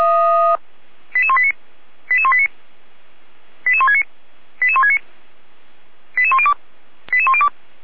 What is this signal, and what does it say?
Five-tone selective call (ZVEI), heard as quick runs of five tones, tones of about 70 ms. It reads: 88189 88189 88169 88169 88102 88102